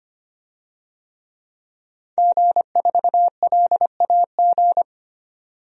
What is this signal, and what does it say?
Morse code, heard as a keyed tone, about 25 wpm, 700 Hz. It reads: G4LAG
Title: DXCC-100 Call Signs - 5 Characters - 25wpm